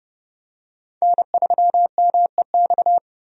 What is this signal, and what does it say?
Morse code, heard as a keyed tone, about 30 wpm, 700 Hz. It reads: N3MEX
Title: USA Call Signs - 5 Characters - 30wpm